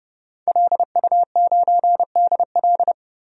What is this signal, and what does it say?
Morse code, heard as a keyed tone, about 30 wpm, 700 Hz. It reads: LU9DL
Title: DXCC-100 Call Signs - 5 Characters - Repeated 2x - 30wpm